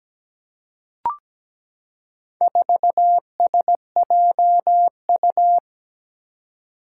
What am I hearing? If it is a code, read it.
Morse code, heard as a keyed tone, about 17 wpm, 700 Hz. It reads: 4SJU
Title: Four Challenging Letter-Number Combinations 17wpm